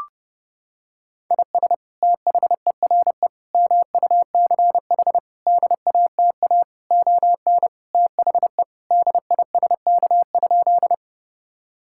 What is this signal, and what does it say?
Morse code, heard as a keyed tone, about 30 wpm, 700 Hz. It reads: IS THERE MUCH DATA ON THE DISK?
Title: Sentences from Top 1000 Words - Review Part 6 - 30wpm